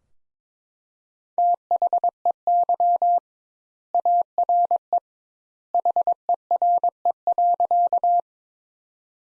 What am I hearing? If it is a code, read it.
Morse code, heard as a keyed tone, about 22 wpm, 700 Hz. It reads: THEY ARE HERE.
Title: Sentences Top 200 Words - 22wpm